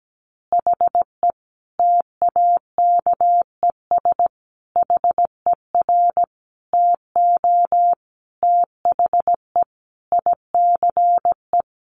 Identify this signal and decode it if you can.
Morse code, heard as a keyed tone, about 17 wpm, 700 Hz. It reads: HE TAKES HER TO THE ICE
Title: Additional Easy Phrases 17wpm